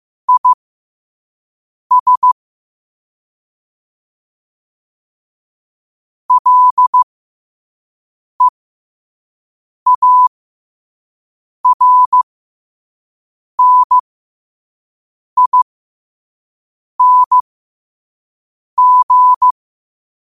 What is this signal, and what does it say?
Morse code, heard as a keyed tone, about 15 wpm, spaced as for 5 wpm, 1000 Hz. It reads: IS LEARNING
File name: qotdfiles / 5wpm